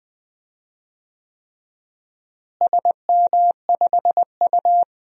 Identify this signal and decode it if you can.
Morse code, heard as a keyed tone, about 20 wpm, 700 Hz. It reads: SM5U
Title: DXCC-100 Call Signs - 4 Characters - Repeated 3x - 20wpm